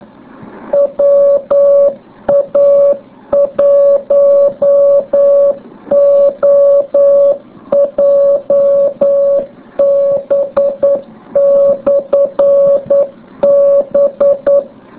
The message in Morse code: WA1OJB/B